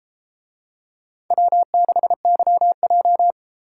Morse code, heard as a keyed tone, about 33 wpm, 700 Hz. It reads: W6YJ